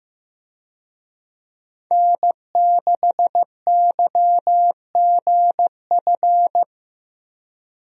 Morse code, heard as a keyed tone, about 15 wpm, 700 Hz. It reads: N6YGF
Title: USA Call Signs - 5 Characters - Repeated 3x - 15wpm